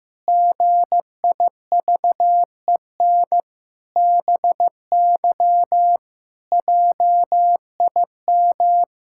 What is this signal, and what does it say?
Morse code, heard as a keyed tone, about 15 wpm, 700 Hz. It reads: GIVEN BY JIM
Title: Pangram Challenge v2 - 15wpm